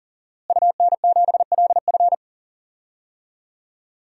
Morse code, heard as a keyed tone, about 40 wpm, 700 Hz. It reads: UN7LF
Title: DXCC-100 Call Signs - 5 Characters - Repeated 3x - 40wpm